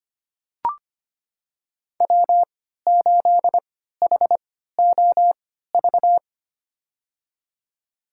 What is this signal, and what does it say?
Morse code, heard as a keyed tone, about 25 wpm, spaced as for 14 wpm, 700 Hz. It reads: W8HOV
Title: USA Call Signs - 3x Character Spacing - 25wpm